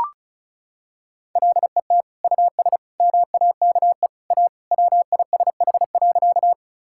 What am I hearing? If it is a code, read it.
Morse code, heard as a keyed tone, about 34 wpm, 700 Hz. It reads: LET US MAKE A WISH.
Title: Sentences from Top 1600 Words 35wpm